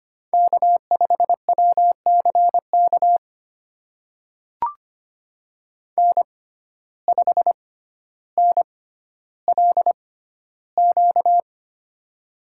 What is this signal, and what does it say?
Morse code, heard as a keyed tone, about 25 wpm, spaced as for 13 wpm, 700 Hz. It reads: K5WCK N5NLQ
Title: USA Call Signs - 5 Characters - 6x Character Spacing - 25wpm